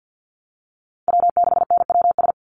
Morse code, heard as a keyed tone, about 50 wpm, 700 Hz. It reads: W6NWS
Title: Call Signs Repeated 2x - 50wpm